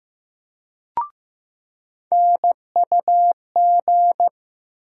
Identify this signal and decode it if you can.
Morse code, heard as a keyed tone, about 15 wpm, 700 Hz. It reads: NUG
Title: USA Call Sign Postfixes - 3 Letters 15wpm